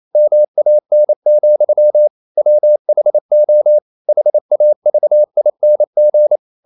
Morse code, heard as a keyed tone, about 28 wpm, 600 Hz. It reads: MAN, WHO HAVING
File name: CW / Practice / EnglishQuotations0228